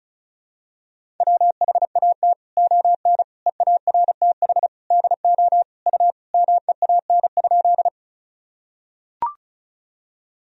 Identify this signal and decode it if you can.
Morse code, heard as a keyed tone, about 34 wpm, 690 Hz. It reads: WHAT ON EARTH DO U MEAN?
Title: Sentences from Top 200 Words and Abbreviations 35wpm